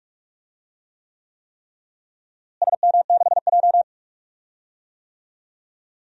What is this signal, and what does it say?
Morse code, heard as a keyed tone, about 46 wpm, 700 Hz. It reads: SM6J
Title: DXCC-100 Call Signs - 4 Characters - Repeated 3x - 45wpm